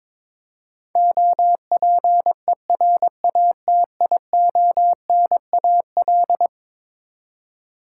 Morse code, heard as a keyed tone, about 22 wpm, 700 Hz. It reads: OPERATIONAL
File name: practice / 22wpm / Eleven Letter Words - 22wpm